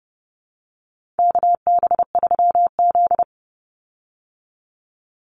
Morse code, heard as a keyed tone, about 30 wpm, 700 Hz. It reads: KB3Z
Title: Call Signs Repeated 2x - 30wpm